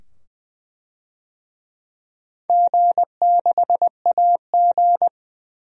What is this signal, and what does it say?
Morse code, heard as a keyed tone, about 20 wpm, 700 Hz. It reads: G6AG
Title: DXCC-100 Call Signs - 4 Characters - 20wpm